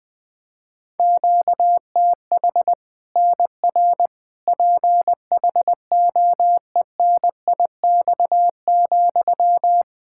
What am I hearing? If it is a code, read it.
Morse code, heard as a keyed tone, about 20 wpm, 700 Hz. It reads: QTH NR PHOENIX,
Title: K1USN SST QSO Practice 20wpm